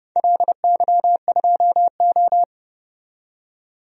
Morse code, heard as a keyed tone, about 30 wpm, 700 Hz. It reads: LY2O
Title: DXCC-100 Call Signs - 4 Characters - 30wpm